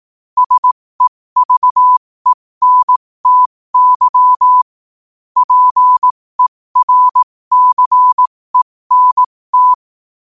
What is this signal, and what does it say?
Morse code, heard as a keyed tone, about 18 wpm, spaced as for 15 wpm, 1000 Hz. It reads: SEVENTY PERCENT
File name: qotdfiles / 15wpm